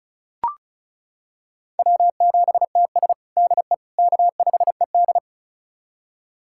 Morse code, heard as a keyed tone, about 34 wpm, 700 Hz. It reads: W7TS DE K5ED
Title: USA Callsign DE Callsign - 4 Characters - 35wpm